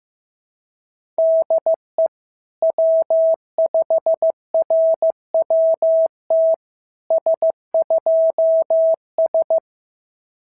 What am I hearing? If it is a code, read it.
Morse code, heard as a keyed tone, about 14 wpm, 650 Hz. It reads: DE W5RWT S2S
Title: SOTA QSO Practice 15wpm